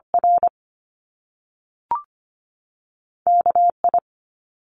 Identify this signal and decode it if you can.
Morse code, heard as a keyed tone, about 25 wpm, 700 Hz. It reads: R KI
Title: Top Two Letter Combinations in Words 25wpm